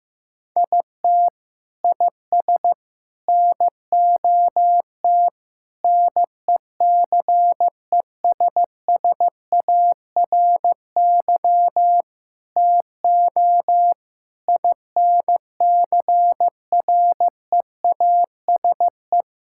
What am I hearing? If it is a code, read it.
Morse code, heard as a keyed tone, about 15 wpm, 700 Hz. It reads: IT IS NOT NECESSARY TO INCREASE
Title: Sentences from Top 1000 Words - Review Part 6 - 15wpm